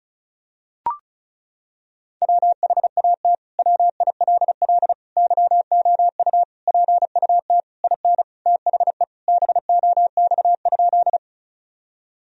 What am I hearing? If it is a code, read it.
Morse code, heard as a keyed tone, about 36 wpm, 700 Hz. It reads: WHAT WILL YOU PUT IN THE BOX?
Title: Sentences from Top 1000 Words - Review Part 2 - 35wpm